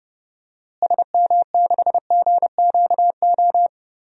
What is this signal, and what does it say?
Morse code, heard as a keyed tone, about 32 wpm, 700 Hz. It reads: SM6GQO
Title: DXCC-100 Call Signs - 6 Characters - Repeated 2x - 30wpm